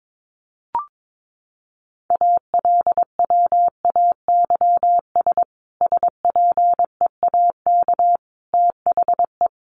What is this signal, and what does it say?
Morse code, heard as a keyed tone, about 22 wpm, 700 Hz. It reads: ALWAYS SPEAK THE
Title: Sentences from Top 1100 Words 22wpm